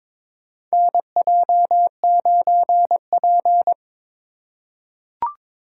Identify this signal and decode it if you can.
Morse code, heard as a keyed tone, about 22 wpm, 700 Hz. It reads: NJ9P
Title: USA Call Signs - 4 Characters - 2x Character Spacing - 22wpm